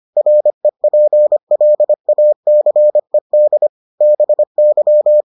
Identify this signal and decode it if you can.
Morse code, heard as a keyed tone, about 25 wpm, 600 Hz. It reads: REPLACED BY